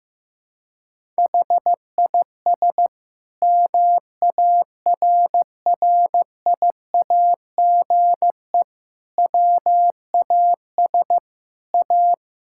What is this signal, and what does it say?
Morse code, heard as a keyed tone, about 15 wpm, 700 Hz. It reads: HIS MARRIAGE WAS A
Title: Sentences from Top 1000 Words - Review Part 5 - 15wpm